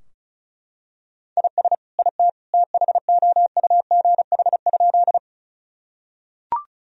Morse code, heard as a keyed tone, about 34 wpm, 700 Hz. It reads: IS IT THOUGH?
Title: Sentences from Top 1000 Words - Review Part 2 - 35wpm